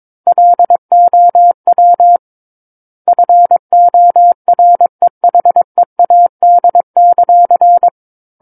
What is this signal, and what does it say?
Morse code, heard as a keyed tone, about 22 wpm, 700 Hz. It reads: LOW FOREHEAD;
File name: Book One / 13-Chapter 13 - 22 WPM